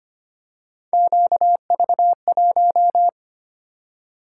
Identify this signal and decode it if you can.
Morse code, heard as a keyed tone, about 25 wpm, 700 Hz. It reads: QV1
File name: Three Letter-Number Combinations - 25wpm